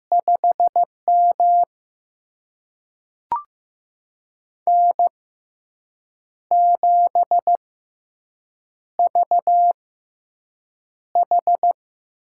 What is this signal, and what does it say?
Morse code, heard as a keyed tone, about 15 wpm, spaced as for 6 wpm, 700 Hz. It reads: 5M N7VH
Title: USA Call Signs - 4 Characters - 6x Character Spacing - 15wpm